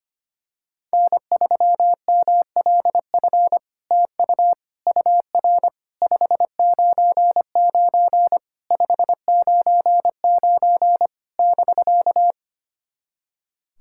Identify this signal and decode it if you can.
Morse code, heard as a keyed tone, about 25 wpm, 700 Hz. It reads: N3MLF TU UR 599 599 <BK>
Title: SOTA QSO Practice 25wpm